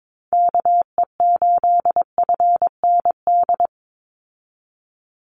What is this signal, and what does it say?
Morse code, heard as a keyed tone, about 22 wpm, 700 Hz. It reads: KE8FND
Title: USA Call Signs 22wpm